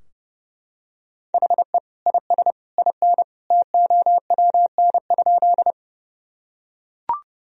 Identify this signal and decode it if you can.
Morse code, heard as a keyed tone, about 32 wpm, 700 Hz. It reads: HE IS IN TOWN?
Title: Sentences from Top 1000 Words - Review Part 2 - 30wpm